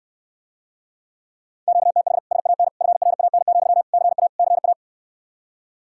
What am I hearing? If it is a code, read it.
Morse code, heard as a keyed tone, about 69 wpm, 700 Hz. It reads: QTH SAN FRAN, CA CA